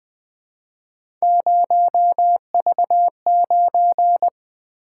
Morse code, heard as a keyed tone, about 21 wpm, 700 Hz. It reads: 0V9
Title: Three Letter-Number Combinations - 20wpm